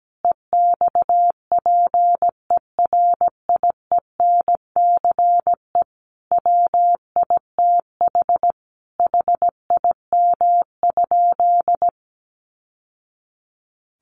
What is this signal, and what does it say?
Morse code, heard as a keyed tone, about 17 wpm, 700 Hz. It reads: EXPERIENCE WITH HIM?
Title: Sentences from Top 600 Words 17wpm